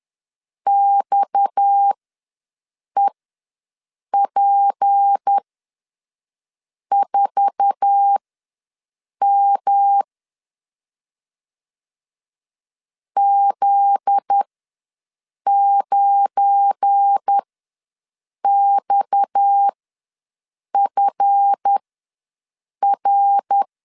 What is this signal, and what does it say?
Morse code, heard as a keyed tone, about 10 wpm, spaced as for 6 wpm, 800 Hz. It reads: XEP4M Z9XFR
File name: LEZIONI / 8 YJ45 / 8 YJ4 ws 10wpm 4020